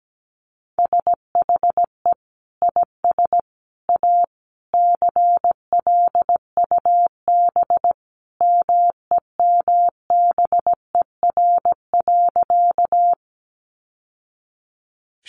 Morse code, heard as a keyed tone, about 17 wpm, 700 Hz. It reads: SHE IS A CLUB MEMBER.
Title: Sentences from Top 1200 Words 17wpm